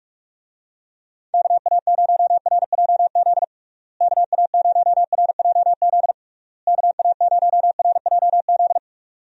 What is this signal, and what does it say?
Morse code, heard as a keyed tone, about 41 wpm, 700 Hz. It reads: KA0RJZ KA0RJZ KA0RJZ
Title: USA Call Signs - 6 Characters - Repeated 3x - 45wpm